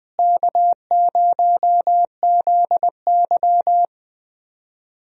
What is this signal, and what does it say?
Morse code, heard as a keyed tone, about 21 wpm, 700 Hz. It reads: K0ZY
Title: USA Call Signs - 4 Characters - Repeated 2x - 20wpm